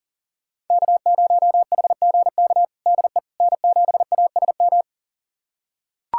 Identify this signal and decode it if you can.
Morse code, heard as a keyed tone, about 40 wpm, 700 Hz. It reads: K0HGK DE N7ASM
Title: USA Callsign DE Callsign - 5 Characters - 40wpm